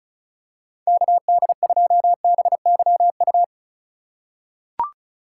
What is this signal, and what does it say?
Morse code, heard as a keyed tone, about 36 wpm, 700 Hz. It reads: KD2BYU
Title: USA Call Signs - 6 Characters - 4x Character Spacing - 35wpm